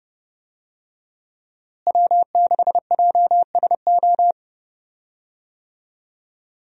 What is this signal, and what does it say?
Morse code, heard as a keyed tone, about 29 wpm, 700 Hz. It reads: W6JSO
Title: USA Call Signs - 5 Characters - Repeated 3x - 30wpm